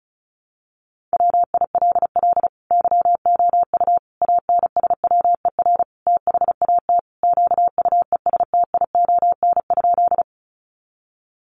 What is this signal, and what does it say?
Morse code, heard as a keyed tone, about 34 wpm, 700 Hz. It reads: WILL YOU ANSWER THAT QUESTION?
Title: Sentences from Top 400 Words 35wpm